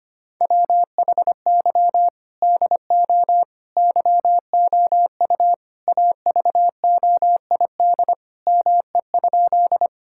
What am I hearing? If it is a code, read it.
Morse code, heard as a keyed tone, about 25 wpm, 700 Hz. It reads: WHY DO YOU AVOID ME?
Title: Sentences from Top 1000 Words - Review Part 6 - 25wpm